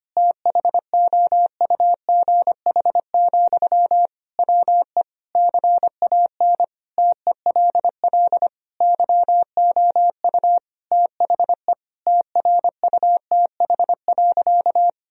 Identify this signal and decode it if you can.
Morse code, heard as a keyed tone, about 25 wpm, 700 Hz. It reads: THOUGH, WE CAN TELL YOU THE TRUTH.
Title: Sentences from Top 1000 Words - Review Part 2 - 25wpm